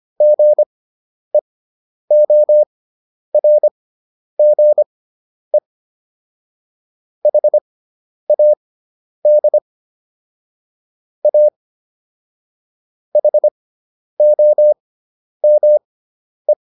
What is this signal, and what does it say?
Morse code, heard as a keyed tone, about 25 wpm, spaced as for 10 wpm, 600 Hz. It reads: GEORGE HAD A HOME